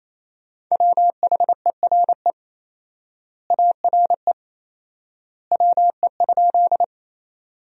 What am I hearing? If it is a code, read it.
Morse code, heard as a keyed tone, about 28 wpm, 700 Hz. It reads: WHERE ARE WE?
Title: 4x Word-Spacing - Sentences from Top 200 Words - 28wpm